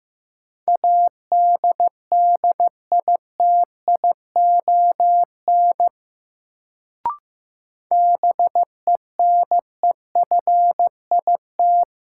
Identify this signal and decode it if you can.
Morse code, heard as a keyed tone, about 15 wpm, 700 Hz. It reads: ADDITION BENEFIT
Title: Sentences from Top 1000 Words - Review Part 6 - New Word Drill - 15wpm